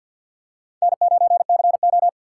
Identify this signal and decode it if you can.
Morse code, heard as a keyed tone, about 47 wpm, 700 Hz. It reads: N9CO
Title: USA Call Signs - 4 Characters - 3x Character Spacing - 50wpm